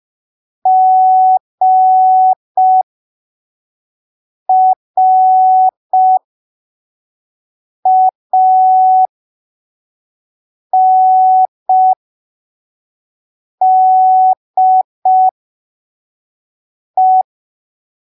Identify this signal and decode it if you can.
Morse code, heard as a keyed tone, about 5 wpm, spaced as for 3 wpm, 750 Hz. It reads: GRANDE
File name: arquivos / texto / texto 22 05ppm